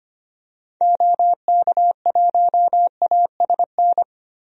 Morse code, heard as a keyed tone, about 25 wpm, 700 Hz. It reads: OK1ASN
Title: DXCC-100 Call Signs - 6 Characters - Repeated 3x - 25wpm